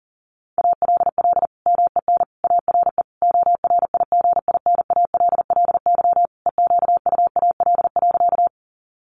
Morse code, heard as a keyed tone, about 40 wpm, 700 Hz. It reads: ALL MEN ARE ORIGINALLY EQUAL.
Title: Sentences from Top 1500 Words 40wpm